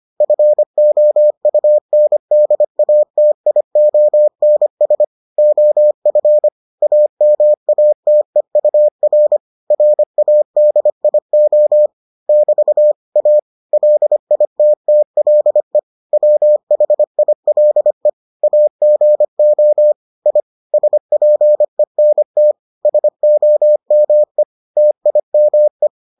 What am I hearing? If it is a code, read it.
Morse code, heard as a keyed tone, about 25 wpm, 600 Hz. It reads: FOUNDATIONS OF AMATEUR RADIO <BT> A LITTLE WHILE AGO I SPENT SOME TIME